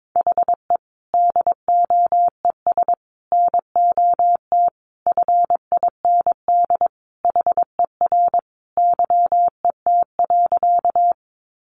Morse code, heard as a keyed tone, about 22 wpm, 700 Hz. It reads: HE DOES NOT FIND HER YET.
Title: Easy Phrases 22wpm